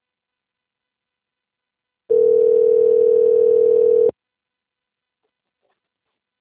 Ringback tone (North America), 440 + 480 Hz.
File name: out- - - - - .3453